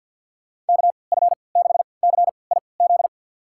Morse code, heard as a keyed tone, about 52 wpm, spaced as for 29 wpm, 700 Hz. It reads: KF6CIZ
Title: USA Call Signs - 6 Characters - 3x Character Spacing - 50wpm